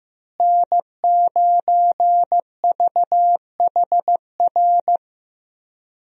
Morse code, heard as a keyed tone, about 16 wpm, 700 Hz. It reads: N9VHR
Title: USA Call Signs - 5 Characters - Repeated 3x - 15wpm